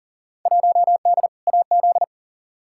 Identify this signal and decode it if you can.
Morse code, heard as a keyed tone, about 40 wpm, 700 Hz. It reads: 1D AZ
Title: Field Day Exchanges - 40wpm